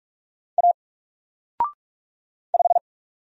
Morse code, heard as a keyed tone, about 44 wpm, 700 Hz. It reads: A 5
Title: Single Character (Letter:Number:Punctuation:Prosign) Alt 45wpm